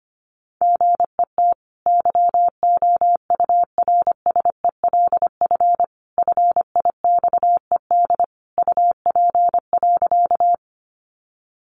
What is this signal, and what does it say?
Morse code, heard as a keyed tone, about 25 wpm, 700 Hz. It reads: GET YOURSELF FIXED UP.
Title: Sentences from Top 1100 Words 25wpm